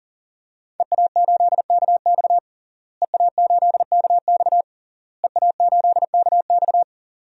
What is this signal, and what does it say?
Morse code, heard as a keyed tone, about 40 wpm, 700 Hz. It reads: EA8KX EA8KX EA8KX
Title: DXCC-100 Call Signs - 5 Characters - Repeated 3x - 40wpm